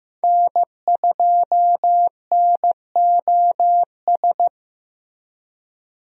Morse code, heard as a keyed tone, about 15 wpm, 700 Hz. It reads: N2NOS
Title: USA Call Signs - 5 Characters - Repeated 3x - 15wpm